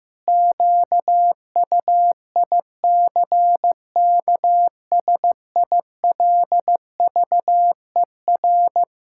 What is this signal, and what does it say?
Morse code, heard as a keyed tone, about 15 wpm, 700 Hz. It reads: QUICKSILVER
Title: Pangram Challenge v2 - 15wpm